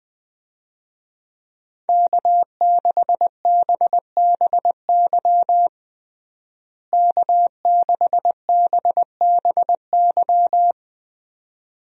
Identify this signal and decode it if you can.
Morse code, heard as a keyed tone, about 21 wpm, 700 Hz. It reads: K6BBY K6BBY
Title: USA Call Signs - 5 Characters - Repeated 2x - 20wpm